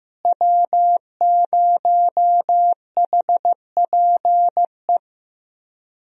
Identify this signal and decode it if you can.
Morse code, heard as a keyed tone, about 15 wpm, 700 Hz. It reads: W0HPE
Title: USA Call Signs - 5 Characters - Repeated 3x - 15wpm